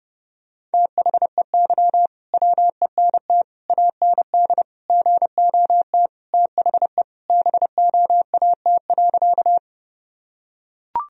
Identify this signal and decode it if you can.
Morse code, heard as a keyed tone, about 30 wpm, 700 Hz. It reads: THEY WENT AND GOT THE BOAT.
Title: Sentences from Top 1000 Words - Review Part 3 - 30wpm